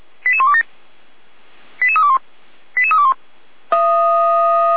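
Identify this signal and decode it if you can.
Five-tone selective call (ZVEI), heard as quick runs of five tones, tones of about 70 ms. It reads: 88127 88321 88321
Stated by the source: ZVEI five-tone selective call received on an SDR.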